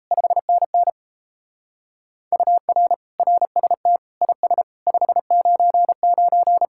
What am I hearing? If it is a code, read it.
Morse code, heard as a keyed tone, about 34 wpm, 700 Hz. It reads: 5NN UR RST IS 599
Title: Speed-Racing Common QSO Phrases - 28wpm